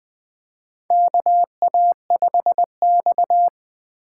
Morse code, heard as a keyed tone, about 20 wpm, 700 Hz. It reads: KA5X
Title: USA Call Signs - 4 Characters - 3x Character Spacing - 20wpm